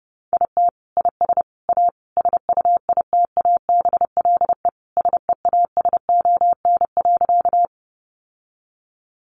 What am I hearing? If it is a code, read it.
Morse code, heard as a keyed tone, about 30 wpm, 700 Hz. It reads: IT IS A SUITABLE SEASON.
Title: Sentences from Top 1400 Words 30wpm